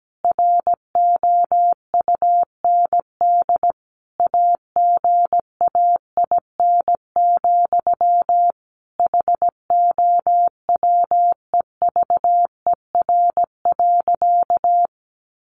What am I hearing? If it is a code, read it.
Morse code, heard as a keyed tone, about 17 wpm, 700 Hz. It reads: ROUND AGAIN, HOWEVER.
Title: Sentences from Top 600 Words 17wpm